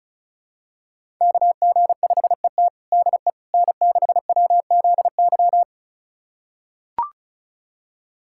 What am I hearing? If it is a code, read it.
Morse code, heard as a keyed tone, about 36 wpm, 700 Hz. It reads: KG5ET DE N6WZY
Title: USA Callsign DE Callsign - 5 Characters - 35wpm